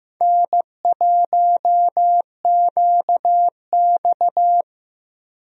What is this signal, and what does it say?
Morse code, heard as a keyed tone, about 15 wpm, 700 Hz. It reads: N1QX
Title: USA Call Signs - 4 Characters - Repeated 3x - 15wpm